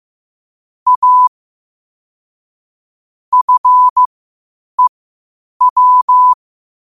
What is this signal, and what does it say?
Morse code, heard as a keyed tone, about 15 wpm, spaced as for 8 wpm, 1000 Hz. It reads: A FEW
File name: qotdfiles / 8wpm